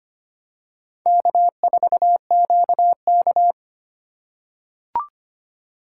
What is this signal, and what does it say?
Morse code, heard as a keyed tone, about 25 wpm, 700 Hz. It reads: K4QK
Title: USA Call Signs - 3x Character Spacing - 25wpm